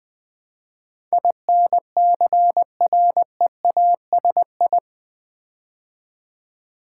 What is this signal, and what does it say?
Morse code, heard as a keyed tone, about 20 wpm, 700 Hz. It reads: INCREASI
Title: Build A Word - 4 Letters At A Time - 12-Letter Words - 20wpm